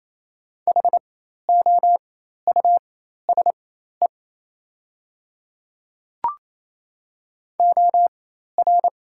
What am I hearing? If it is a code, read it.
Morse code, heard as a keyed tone, about 28 wpm, spaced as for 13 wpm, 690 Hz. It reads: HOUSE OR
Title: 4x Character Spacing - Top 300 Words - 28wpm